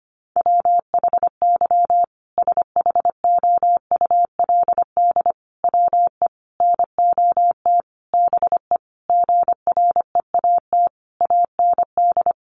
Morse code, heard as a keyed tone, about 25 wpm, 700 Hz. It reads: WHY SHOULD WE NOT BE GREAT AND
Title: Sentences from Top 1500 Words 25wpm